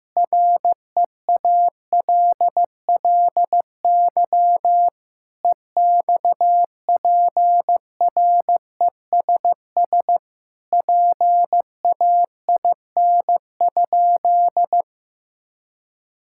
Morse code, heard as a keyed tone, about 15 wpm, 700 Hz. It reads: REALLY EXPRESS PAIN?
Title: Sentences from Top 1600 Words 15wpm